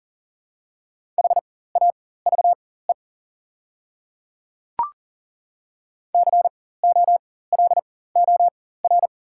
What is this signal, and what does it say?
Morse code, heard as a keyed tone, about 40 wpm, spaced as for 19 wpm, 700 Hz. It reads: HAVE COLOR
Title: 4x Character Spacing - Top 300 Words - 40wpm